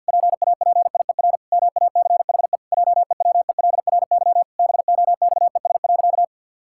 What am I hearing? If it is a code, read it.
Morse code, heard as a keyed tone, about 50 wpm, 700 Hz. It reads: PAPIER MACHE JEWELRY BOXES.
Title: Pangram Challenge v2 - 50wpm